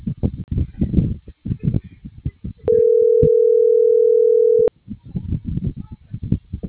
Ringback tone (North America), 440 + 480 Hz.